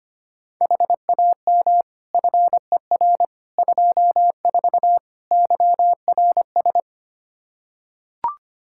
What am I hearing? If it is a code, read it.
Morse code, heard as a keyed tone, about 25 wpm, 700 Hz. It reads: HAM FER 24 YRS
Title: Practice QSO - Exchange 3 - More Info - 25wpm